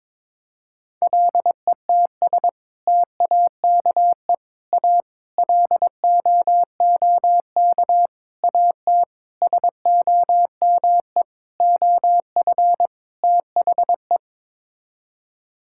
Morse code, heard as a keyed tone, about 22 wpm, 700 Hz. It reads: LETS TAKE A LOOK AT SOME OF THE
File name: practice / 22wpm / Sets of 8 Words - 22wpm